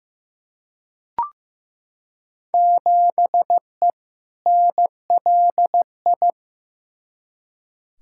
Morse code, heard as a keyed tone, about 15 wpm, 700 Hz. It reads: 7E NLI
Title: Field Day Exchanges Encore - 15wpm